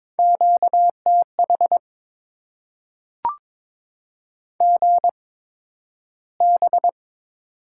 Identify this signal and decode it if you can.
Morse code, heard as a keyed tone, about 22 wpm, spaced as for 12 wpm, 700 Hz. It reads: QTH GB
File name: Top 90 QSO Elements - 8x Character Spacing - 22wpm